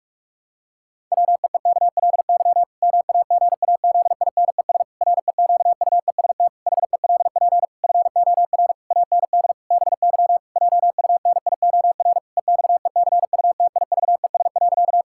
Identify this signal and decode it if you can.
Morse code, heard as a keyed tone, about 46 wpm, 700 Hz. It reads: WEEKLY MAGAZINES REQUEST HELP FOR AND BY JUNIOR EXECUTIVES.
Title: Pangram Challenge v2 - 45wpm